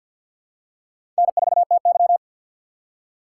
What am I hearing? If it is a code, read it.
Morse code, heard as a keyed tone, about 50 wpm, 700 Hz. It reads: N4TY
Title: USA Call Signs - 3x Character Spacing - 50wpm